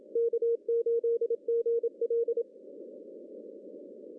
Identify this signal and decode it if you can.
Morse code, heard as a keyed tone, about 27 wpm, 460 Hz. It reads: K8GL
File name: CW / cqww-cw-k5tr